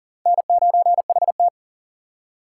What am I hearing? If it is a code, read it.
Morse code, heard as a keyed tone, about 55 wpm, spaced as for 47 wpm, 700 Hz. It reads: N9HT
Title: Speed Racing - USA 1x2 Call Signs 40wpm